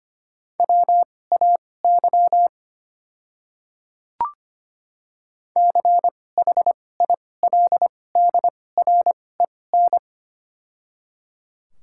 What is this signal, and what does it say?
Morse code, heard as a keyed tone, about 25 wpm, spaced as for 18 wpm, 700 Hz. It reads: WAY CHILDREN
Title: 2x Character Spacing - Top 300 Words - 25wpm